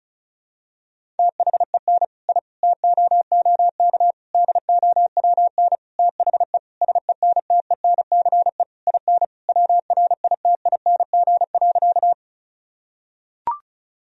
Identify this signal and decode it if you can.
Morse code, heard as a keyed tone, about 34 wpm, 700 Hz. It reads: THEN I TOOK DOWN THE SENTENCE IN WRITING.
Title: Sentences from Top 1000 Words - Review Part 1 - 35wpm